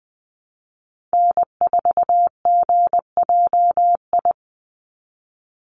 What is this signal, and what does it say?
Morse code, heard as a keyed tone, about 20 wpm, 700 Hz. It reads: N4GJI
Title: USA Call Signs 20wpm